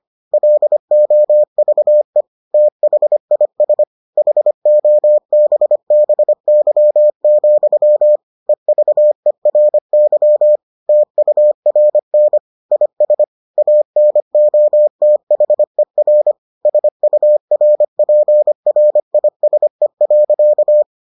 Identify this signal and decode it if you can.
Morse code, heard as a keyed tone, about 25 wpm, 600 Hz. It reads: LOVE THIS HOBBY, EVERY TURN IS ANOTHER SURPRISE.